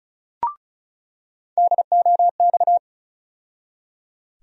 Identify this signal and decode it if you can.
Morse code, heard as a keyed tone, about 36 wpm, 700 Hz. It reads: DOX
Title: USA Call Sign Postfixes - 3 Letters 35wpm